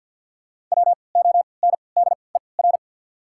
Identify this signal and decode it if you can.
Morse code, heard as a keyed tone, about 54 wpm, spaced as for 29 wpm, 700 Hz. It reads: WONDER
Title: 3x Character Spacing - Top 500 Words - 50wpm